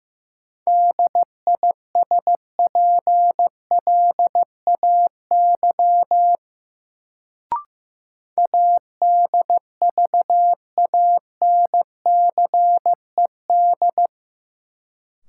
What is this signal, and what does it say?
Morse code, heard as a keyed tone, about 15 wpm, 700 Hz. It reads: DISPLAY ADVANCED
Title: Sentences from Top 1200 Words - New Word Drill - 15wpm